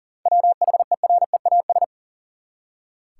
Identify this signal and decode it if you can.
Morse code, heard as a keyed tone, about 40 wpm, 700 Hz. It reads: WHEREAS
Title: Seven Letter Words - 40wpm